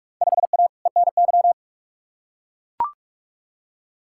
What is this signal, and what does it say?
Morse code, heard as a keyed tone, about 41 wpm, 700 Hz. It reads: 5A ENY
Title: Field Day Exchanges - 45wpm